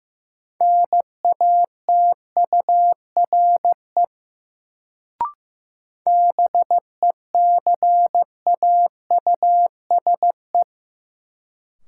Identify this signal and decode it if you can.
Morse code, heard as a keyed tone, about 15 wpm, 700 Hz. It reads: NATURE BECAUSE
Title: Sentences from Top 600 Words - New Word Drill - 15wpm